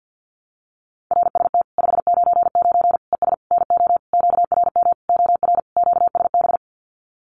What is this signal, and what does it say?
Morse code, heard as a keyed tone, about 50 wpm, 700 Hz. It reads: RST 599 ES NO QRM OR QSB
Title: Common QSO Phrases 50wpm